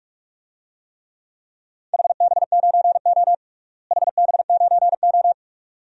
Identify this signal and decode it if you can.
Morse code, heard as a keyed tone, about 46 wpm, 700 Hz. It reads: HB9O HB9O
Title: DXCC-100 Call Signs - 4 Characters - Repeated 2x - 45wpm